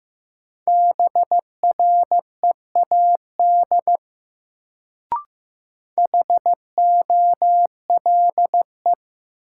Morse code, heard as a keyed tone, about 15 wpm, 700 Hz. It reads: BREAD HOLE
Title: Sentences from Top 1600 Words - New Word Drill Alt - 15wpm